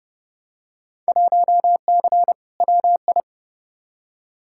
Morse code, heard as a keyed tone, about 31 wpm, 700 Hz. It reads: 1C WI
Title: Field Day Exchanges - 30wpm